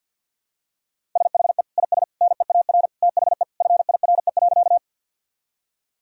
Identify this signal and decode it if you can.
Morse code, heard as a keyed tone, about 47 wpm, 700 Hz. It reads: SHE IS NEAR THE FIRE.
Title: Sentences from Top 1000 Words - Review Part 1 - 50wpm